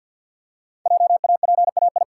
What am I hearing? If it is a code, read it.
Morse code, heard as a keyed tone, about 50 wpm, 700 Hz. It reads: JAPAI
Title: Code Groups - 5 Letter:Numbers - 50wpm